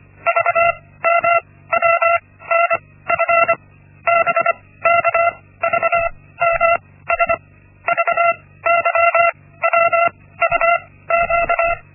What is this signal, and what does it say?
Morse code, heard as a keyed tone, about 25 wpm, spaced as for 16 wpm, 680 Hz. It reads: VMWNF BKVMS VYWUQ
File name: Linksmakalnis-1